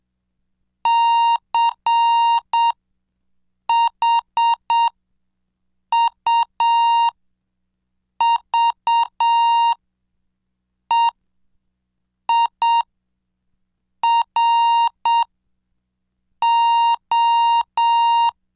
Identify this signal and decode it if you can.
Morse code, heard as a keyed tone, about 7 wpm, spaced as for 5 wpm, 930 Hz. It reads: CHUVEIRO
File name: cursos / CDestudo / 17 Track 17